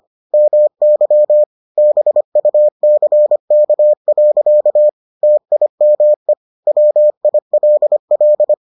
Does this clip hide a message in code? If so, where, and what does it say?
Yes: Morse code from 0.3 to 8.5 s, heard as a keyed tone, about 25 wpm, 600 Hz. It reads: MY BUCK. TIME WILL